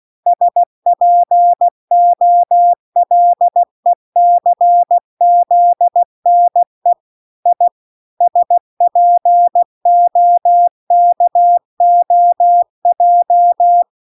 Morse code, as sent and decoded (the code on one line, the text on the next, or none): ... .--. --- .-.. -.-. --.. -. / .. / ... .--. --- -.- --- .---
SPOLECZNE I SPOKOJ